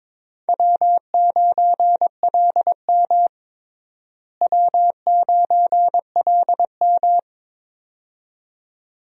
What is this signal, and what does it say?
Morse code, heard as a keyed tone, about 22 wpm, 700 Hz. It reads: W9LM W9LM
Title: USA Call Signs - 4 Characters - Repeated 2x - 22wpm